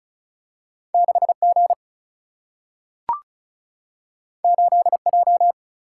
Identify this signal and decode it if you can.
Morse code, heard as a keyed tone, about 36 wpm, 700 Hz. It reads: 6G 8J
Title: Single-Letter Single-Number 35wpm